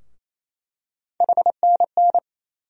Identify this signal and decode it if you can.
Morse code, heard as a keyed tone, about 27 wpm, 700 Hz. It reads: HNN
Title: USA Call Sign Postfixes - 3 Letters 28wpm